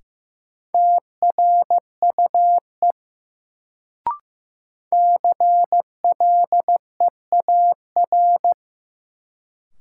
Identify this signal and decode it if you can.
Morse code, heard as a keyed tone, about 15 wpm, 700 Hz. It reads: TRUE CLEAR
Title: Sets of 5 Words Encore - New Word Drill - 15wpm